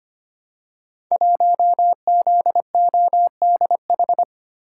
Morse code, heard as a keyed tone, about 25 wpm, 700 Hz. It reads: 1ZODH
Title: Code Groups - 5 Letter:Numbers - 25wpm